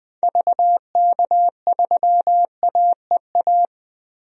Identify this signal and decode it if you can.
Morse code, heard as a keyed tone, about 20 wpm, 700 Hz. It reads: VK3AEA
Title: DXCC-100 Call Signs - 6 Characters - Repeated 2x - 20wpm